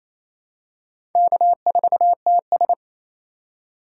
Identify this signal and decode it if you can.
Morse code, heard as a keyed tone, about 28 wpm, 700 Hz. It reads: K4TS
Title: USA Call Signs - 4 Characters - 2x Character Spacing - 28wpm